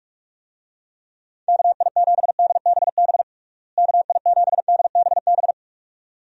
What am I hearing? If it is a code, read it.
Morse code, heard as a keyed tone, about 46 wpm, 700 Hz. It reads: KI7DBB KI7DBB
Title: USA Call Signs - 6 Characters - Repeated 2x - 45wpm